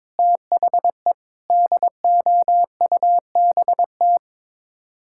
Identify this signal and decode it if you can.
Morse code, heard as a keyed tone, about 22 wpm, 700 Hz. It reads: THE DOUBT
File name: practice / 22wpm / Sets of 8 Words - 22wpm